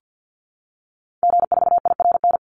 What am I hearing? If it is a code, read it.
Morse code, heard as a keyed tone, about 50 wpm, 700 Hz. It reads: G4IRN
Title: Call Signs Repeated 3x - 50wpm